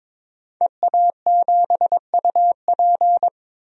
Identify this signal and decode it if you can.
Morse code, heard as a keyed tone, about 22 wpm, 700 Hz. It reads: EA7UP
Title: DXCC-100 Call Signs - 5 Characters - 22wpm